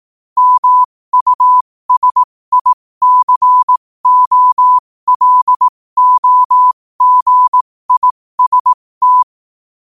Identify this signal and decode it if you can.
Morse code, heard as a keyed tone, about 18 wpm, spaced as for 15 wpm, 1000 Hz. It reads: MUSICOLOGIST
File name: qotdfiles / 15wpm